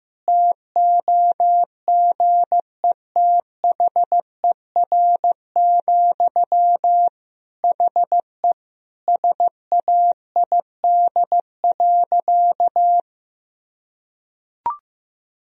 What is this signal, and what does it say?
Morse code, heard as a keyed tone, about 15 wpm, 700 Hz. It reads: TOGETHER, HE SAID.
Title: Sentences from Top 1000 Words - Review Part 2 - 15wpm